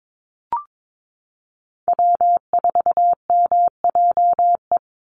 Morse code, heard as a keyed tone, about 22 wpm, 700 Hz. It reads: W4MJE
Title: Call Signs Repeated 2x - 22wpm